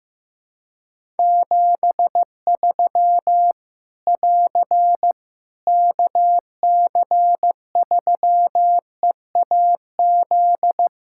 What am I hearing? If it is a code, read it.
Morse code, heard as a keyed tone, about 15 wpm, 700 Hz. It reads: 73 <AR> KC3EAZ
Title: Practice QSO - Exchange 4 - Ending - 15wpm